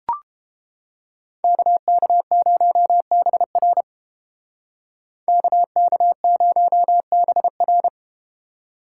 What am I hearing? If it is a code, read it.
Morse code, heard as a keyed tone, about 31 wpm, 700 Hz. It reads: KK0BR KK0BR